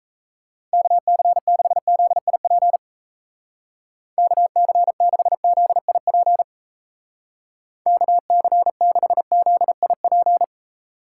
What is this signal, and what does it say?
Morse code, heard as a keyed tone, about 35 wpm, 700 Hz. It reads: KC6ZIP KC6ZIP KC6ZIP
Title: Speed Racing - USA 2x3 Call Signs 28wpm